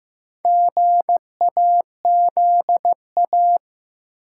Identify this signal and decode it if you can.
Morse code, heard as a keyed tone, about 15 wpm, 700 Hz. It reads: GAZA
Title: Sets of 5 Words - New Word Drill - 15wpm